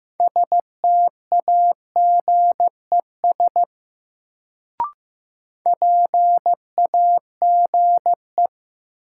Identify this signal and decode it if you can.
Morse code, heard as a keyed tone, about 15 wpm, 700 Hz. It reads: STAGES PAGE
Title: Sets of 5 Words Encore - New Word Drill - 15wpm